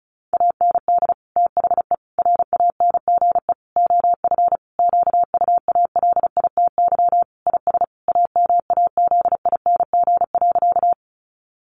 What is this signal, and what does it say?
Morse code, heard as a keyed tone, about 34 wpm, 700 Hz. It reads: AND THE RANGE OF QUALITY IS AMAZING.
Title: Sentences from Top 1500 Words 35wpm